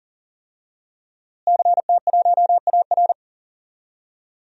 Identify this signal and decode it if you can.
Morse code, heard as a keyed tone, about 40 wpm, 700 Hz. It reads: CT1AR
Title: DXCC-100 Call Signs - 5 Characters - Repeated 3x - 40wpm